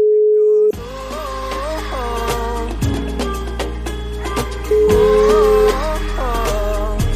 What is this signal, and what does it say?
Ringback tone (Europe, 425 Hz), with noise.